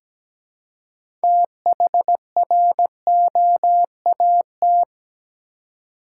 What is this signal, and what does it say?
Morse code, heard as a keyed tone, about 17 wpm, 700 Hz. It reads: THROAT
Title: Six Letter Words - 17wpm